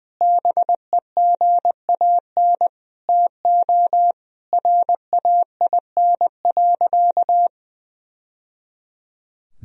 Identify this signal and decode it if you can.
Morse code, heard as a keyed tone, about 20 wpm, 700 Hz. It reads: BEGAN TO RAIN.
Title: Sentences from Top 1000 Words - Review Part 1 - 20wpm